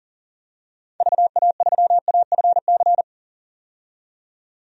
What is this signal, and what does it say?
Morse code, heard as a keyed tone, about 37 wpm, 700 Hz. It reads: VA3AFC
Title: DXCC-100 Call Signs - 6 Characters - 40wpm